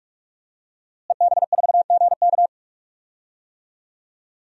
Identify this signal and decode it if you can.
Morse code, heard as a keyed tone, about 45 wpm, 700 Hz. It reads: EB4GK
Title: DXCC-100 Call Signs - 5 Characters - Repeated 2x - 45wpm